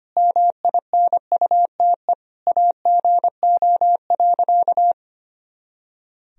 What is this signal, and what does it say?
Morse code, heard as a keyed tone, about 25 wpm, 700 Hz. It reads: MINUTE AGO.
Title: Sentences from Top 500 Words and Abbreviations 25wpm